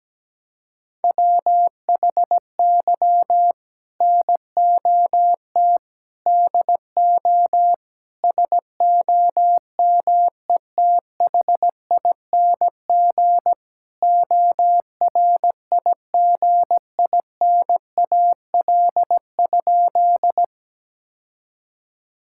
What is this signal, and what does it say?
Morse code, heard as a keyed tone, about 17 wpm, 700 Hz. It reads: WHY NOT DO SOMETHING ORIGINAL?
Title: Sentences from Top 1000 Words - Review Part 6 - 17wpm